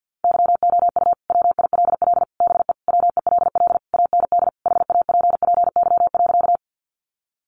Morse code, heard as a keyed tone, about 50 wpm, 700 Hz. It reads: YOU WILL BE WELL AND HAPPY.
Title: Sentences from Top 1000 Words 50wpm